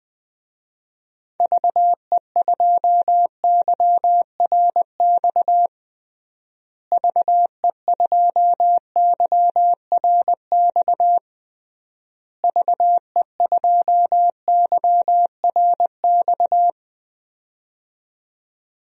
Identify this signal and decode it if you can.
Morse code, heard as a keyed tone, about 21 wpm, 700 Hz. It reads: VE2YRX VE2YRX VE2YRX
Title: DXCC-100 Call Signs - 6 Characters - Repeated 3x - 20wpm